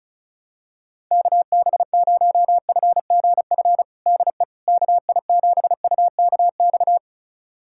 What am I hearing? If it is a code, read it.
Morse code, heard as a keyed tone, about 34 wpm, 700 Hz. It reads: KB0FGF DE KI7UKX
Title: USA Callsign DE Callsign - 6 Characters - 35wpm